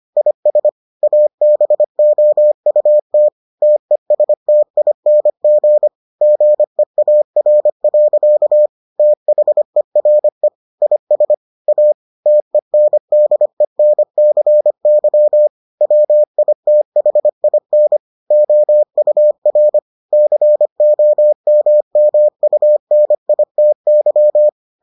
Morse code, heard as a keyed tone, about 25 wpm, 600 Hz. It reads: IS ABOUT TESTING GEAR. THERE IS A TENDENCY WITHIN OUR COMMUNITY